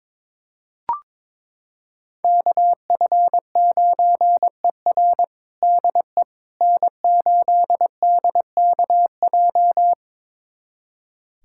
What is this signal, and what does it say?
Morse code, heard as a keyed tone, about 21 wpm, 700 Hz. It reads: KF9ER DE N8DKJ
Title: USA Callsign DE Callsign 22wpm